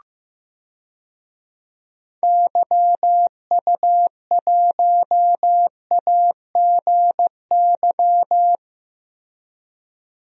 Morse code, heard as a keyed tone, about 15 wpm, 700 Hz. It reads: YU1AGY
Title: DXCC-100 Call Signs - 6 Characters - 15wpm